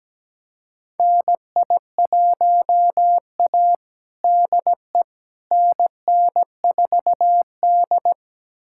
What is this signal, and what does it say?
Morse code, heard as a keyed tone, about 17 wpm, 700 Hz. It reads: NI1A DE NN4D
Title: USA Callsign DE Callsign - 4 Characters - 17wpm